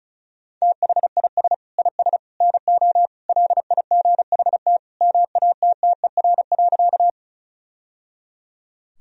Morse code, heard as a keyed tone, about 34 wpm, 700 Hz. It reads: THIS IS NO LIGHT MATTER.
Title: Sentences from Top 1000 Words - Review Part 1 - 35wpm